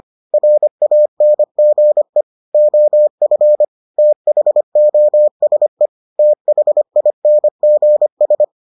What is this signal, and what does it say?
Morse code, heard as a keyed tone, about 25 wpm, 600 Hz. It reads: RANGE OF THOSE THINGS